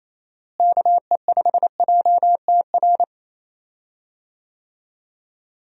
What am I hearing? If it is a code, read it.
Morse code, heard as a keyed tone, about 27 wpm, 700 Hz. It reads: KE5JTR
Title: USA Call Signs - 6 Characters - Repeated 3x - 28wpm